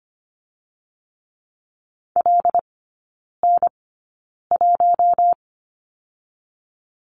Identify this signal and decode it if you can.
Morse code, heard as a keyed tone, about 25 wpm, spaced as for 9 wpm, 700 Hz. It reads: LN1
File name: Lesson 15e - Number 5 - Three Characters 25wpm